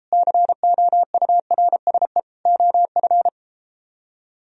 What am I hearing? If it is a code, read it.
Morse code, heard as a keyed tone, about 33 wpm, 700 Hz. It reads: COURSE OF